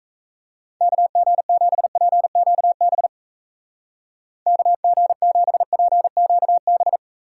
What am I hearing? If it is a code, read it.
Morse code, heard as a keyed tone, about 36 wpm, 700 Hz. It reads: KG7PQB KG7PQB